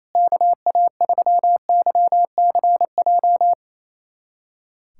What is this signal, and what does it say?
Morse code, heard as a keyed tone, about 27 wpm, 700 Hz. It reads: KA3YCJ
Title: USA Callsign DE Callsign 28wpm